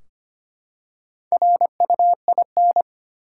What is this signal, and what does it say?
Morse code, heard as a keyed tone, about 25 wpm, 700 Hz. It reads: RUIN
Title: Four Letter Words - 25wpm